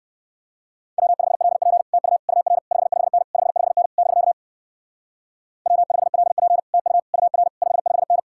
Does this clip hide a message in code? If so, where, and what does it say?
Yes: Morse code from 1.0 to 8.2 s, heard as a keyed tone, about 56 wpm, 700 Hz. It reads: W5LP TU UR 55N 55N <BK> W5LP TU UR 55N